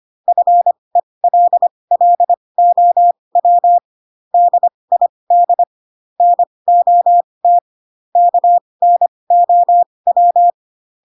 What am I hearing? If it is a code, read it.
Morse code, heard as a keyed tone, about 25 wpm, spaced as for 20 wpm, 700 Hz. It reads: FELLOW DID NOT KNOW